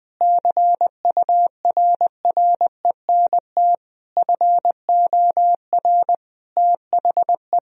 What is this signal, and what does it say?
Morse code, heard as a keyed tone, about 20 wpm, 700 Hz. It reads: CURRENT FOR THE
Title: Sentences from Top 1000 Words - Review Part 4 - 20wpm